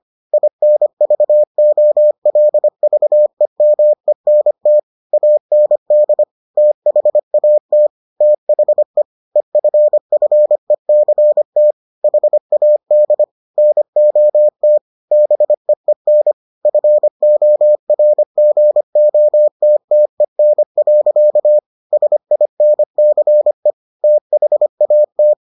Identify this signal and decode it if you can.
Morse code, heard as a keyed tone, about 25 wpm, 600 Hz. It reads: INVOLVEMENT AND THAT THE EFFECT HAD NOT BEEN FORGOTTEN. SINCE THAT